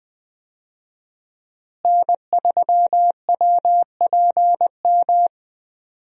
Morse code, heard as a keyed tone, about 20 wpm, 700 Hz. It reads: N3WPM
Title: USA Call Signs - 5 Characters - Repeated 3x - 20wpm